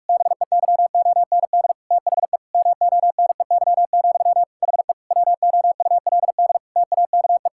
Morse code, heard as a keyed tone, about 46 wpm, 700 Hz. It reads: BEYOND THE MONEY, HE WOULD TAKE
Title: Sentences from Top 1000 Words - Review Part 6 - 45wpm